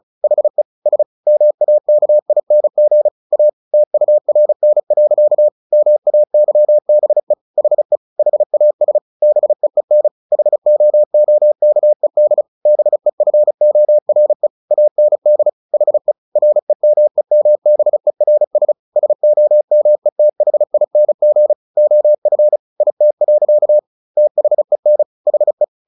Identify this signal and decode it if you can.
Morse code, heard as a keyed tone, about 35 wpm, 600 Hz. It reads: HE S MAKING A TURN. MAYBE HE HAS BEEN HOOKED BEFORE AND HE REMEMBERS SOMETHING OF IT. THEN HE
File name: the old man and the sea / the old man and the sea 1 35wpm